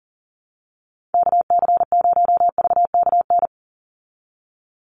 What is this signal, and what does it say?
Morse code, heard as a keyed tone, about 40 wpm, 700 Hz. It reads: KC0VKN
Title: Call Signs Repeated 2x - 40wpm